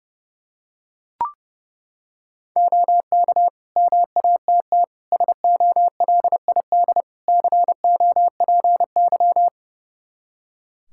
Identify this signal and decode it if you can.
Morse code, heard as a keyed tone, about 30 wpm, 700 Hz. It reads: OK MATT SOLID COPY
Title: Practice QSO - Exchange 3 - More Info - 30wpm